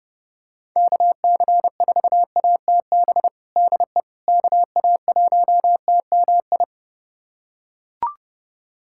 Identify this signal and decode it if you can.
Morse code, heard as a keyed tone, about 30 wpm, 700 Hz. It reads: KC4ATB DE KA1TMI
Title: USA Callsign DE Callsign - 6 Characters - 30wpm